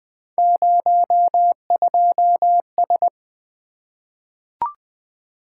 Morse code, heard as a keyed tone, about 20 wpm, 700 Hz. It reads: 02S
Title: Three Letter-Number Combinations - 20wpm